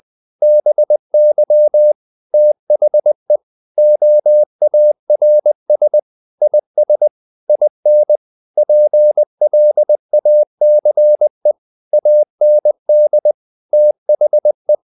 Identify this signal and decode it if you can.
Morse code, heard as a keyed tone, about 20 wpm, 600 Hz. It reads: BY THE OARS IS IN PLACE AND THE
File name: the old man and the sea part 2 20wpm